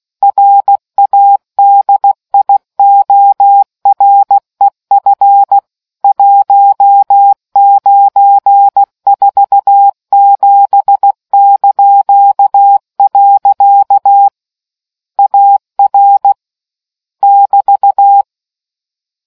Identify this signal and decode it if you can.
Morse code, heard as a keyed tone, about 16 wpm, 800 Hz. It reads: RADIOREF 1947). AR <BT>